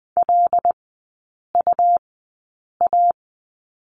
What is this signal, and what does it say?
Morse code, heard as a keyed tone, about 20 wpm, spaced as for 8 wpm, 700 Hz. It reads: LUA
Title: 100 Lesson 17e - Letter u - Three Characters 20wpm